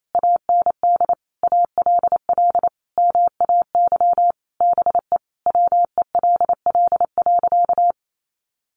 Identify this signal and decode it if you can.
Morse code, heard as a keyed tone, about 28 wpm, 700 Hz. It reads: AND ALL MAY BE WELL.
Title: Sentences from Top 1200 Words 28wpm